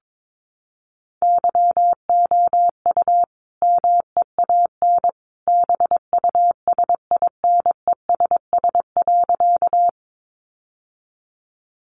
Morse code, heard as a keyed tone, about 22 wpm, 700 Hz. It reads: YOU MEAN BUSINESS.
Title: Sentences from Top 600 Words 22wpm